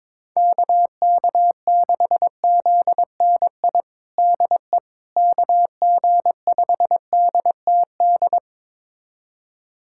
Morse code, heard as a keyed tone, about 22 wpm, 700 Hz. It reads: KK6ZNI DE KG5DTD
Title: USA Callsign DE Callsign - 6 Characters - 22wpm